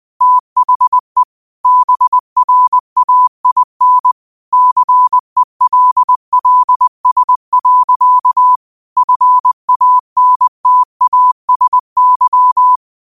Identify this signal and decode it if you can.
Morse code, heard as a keyed tone, about 20 wpm, 1000 Hz. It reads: THE BRAIN CELLS. FANTASY
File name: qotdfiles / 20wpm